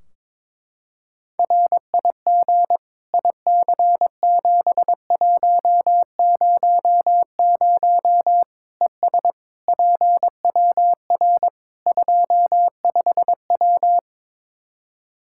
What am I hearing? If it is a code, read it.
Morse code, heard as a keyed tone, about 22 wpm, 700 Hz. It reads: RIG IC7100 ES PWR 25W
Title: K1USN SST QSO Practice 22wpm